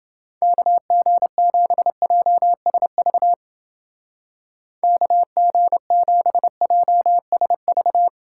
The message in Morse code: KG7JSV KG7JSV

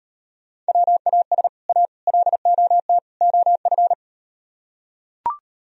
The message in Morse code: WAS A LOT OF

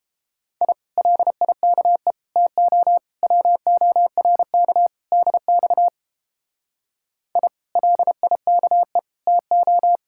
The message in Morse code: I LIKE TO WORK DX I LIKE TO

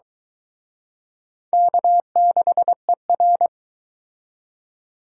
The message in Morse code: K6ER

